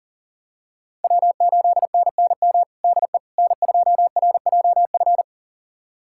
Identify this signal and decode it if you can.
Morse code, heard as a keyed tone, about 40 wpm, 700 Hz. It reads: W8NNM DE N2RJF
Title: USA Callsign DE Callsign - 5 Characters - 40wpm